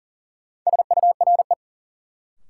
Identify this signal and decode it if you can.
Morse code, heard as a keyed tone, about 40 wpm, 700 Hz. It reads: SURE
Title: Sets of 3 Words - New Word Drill - 40wpm